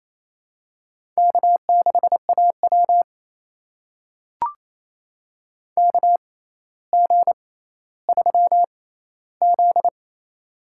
Morse code, heard as a keyed tone, about 28 wpm, spaced as for 14 wpm, 700 Hz. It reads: K6AW KG3Z
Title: USA Call Signs - 6x Character Spacing - 28wpm